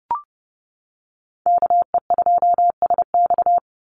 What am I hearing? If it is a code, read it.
Morse code, heard as a keyed tone, about 30 wpm, 700 Hz. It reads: KE2SX
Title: Call Signs - 30wpm